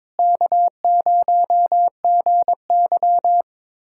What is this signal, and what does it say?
Morse code, heard as a keyed tone, about 22 wpm, 700 Hz. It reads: K0GY
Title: USA Callsign DE Callsign - 4 Characters - 22wpm